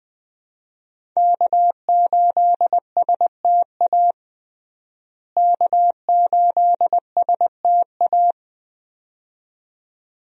Morse code, heard as a keyed tone, about 21 wpm, 700 Hz. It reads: K8STA K8STA
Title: USA Call Signs - 5 Characters - Repeated 2x - 20wpm